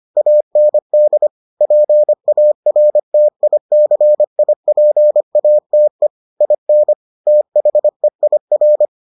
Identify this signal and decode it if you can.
Morse code, heard as a keyed tone, about 25 wpm, 600 Hz. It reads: AND PARTICIPATE IN THEIR